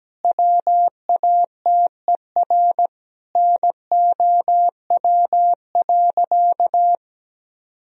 Morse code, heard as a keyed tone, about 17 wpm, 700 Hz. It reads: WATER NOW.
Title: Sentences from Top 1000 Words - Review Part 1 - 17wpm